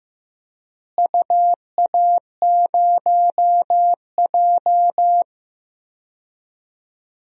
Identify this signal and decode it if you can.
Morse code, heard as a keyed tone, about 15 wpm, 700 Hz. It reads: UA0J
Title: DXCC-100 Call Signs - 4 Characters - Repeated 3x - 15wpm